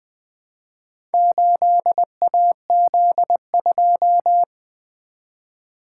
Morse code, heard as a keyed tone, about 20 wpm, 700 Hz. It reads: 8AZ2
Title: Four Letter-Number Combinations - 20wpm